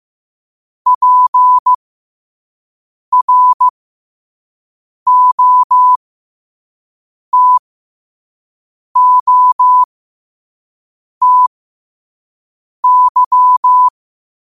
Morse code, heard as a keyed tone, about 15 wpm, spaced as for 5 wpm, 1000 Hz. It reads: PROTOTY
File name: qotdfiles / 5wpm